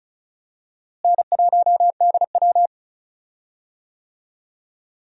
Morse code, heard as a keyed tone, about 36 wpm, 700 Hz. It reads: N1DW
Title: USA Call Signs - 4 Characters - Repeated 2x - 35wpm